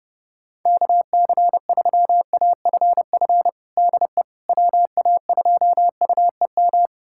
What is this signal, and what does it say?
Morse code, heard as a keyed tone, about 32 wpm, 700 Hz. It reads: KC3AFF DE WA2UEM
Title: USA Callsign DE Callsign - 6 Characters - 30wpm